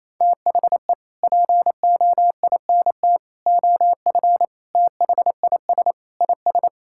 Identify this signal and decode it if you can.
Morse code, heard as a keyed tone, about 27 wpm, 700 Hz. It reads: THE POINT OF THIS IS
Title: Sentences from Top 1600 Words 28wpm